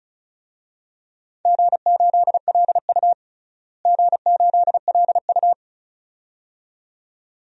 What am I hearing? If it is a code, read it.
Morse code, heard as a keyed tone, about 34 wpm, 700 Hz. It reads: G8LU G8LU
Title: DXCC-100 Call Signs - 4 Characters - Repeated 2x - 35wpm